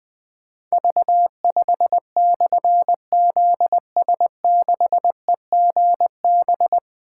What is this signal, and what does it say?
Morse code, heard as a keyed tone, about 20 wpm, 700 Hz. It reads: V5/ZS6EGB
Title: Difficult Call Signs 20wpm